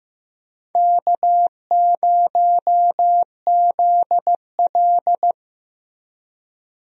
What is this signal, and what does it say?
Morse code, heard as a keyed tone, about 15 wpm, 700 Hz. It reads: K0ZL
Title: USA Call Signs - 4 Characters - Repeated 2x - 15wpm